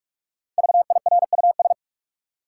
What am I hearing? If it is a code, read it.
Morse code, heard as a keyed tone, about 46 wpm, 700 Hz. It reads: VIRUS
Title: Five Letter Words - 45wpm